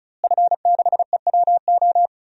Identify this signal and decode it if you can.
Morse code, heard as a keyed tone, about 34 wpm, 700 Hz. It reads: F6EWO
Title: DXCC-100 Call Signs - 5 Characters - Repeated 3x - 35wpm